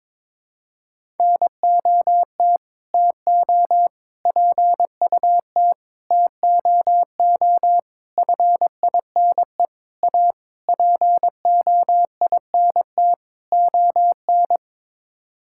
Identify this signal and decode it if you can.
Morse code, heard as a keyed tone, about 22 wpm, 700 Hz. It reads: NOT TO PUT TOO FINE A POINT ON
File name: practice / 22wpm / Sets of 8 Words - 22wpm